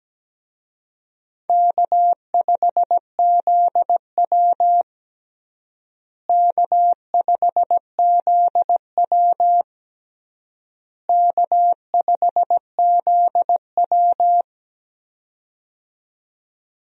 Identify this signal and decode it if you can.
Morse code, heard as a keyed tone, about 17 wpm, 700 Hz. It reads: K5ZW K5ZW K5ZW
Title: USA Call Signs - 4 Characters - Repeated 3x - 17wpm